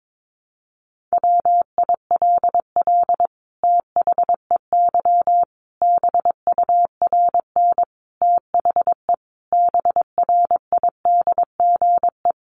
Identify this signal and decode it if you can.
Morse code, heard as a keyed tone, about 22 wpm, 700 Hz. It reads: WILL THEY BURN THE BRIDGE
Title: Sentences from Top 1300 Words 22wpm